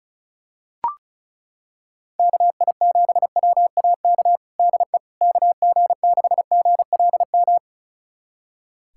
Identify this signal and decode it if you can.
Morse code, heard as a keyed tone, about 34 wpm, 700 Hz. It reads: KI7WAK DE KG6GLM
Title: USA Callsign DE Callsign - 6 Characters - 35wpm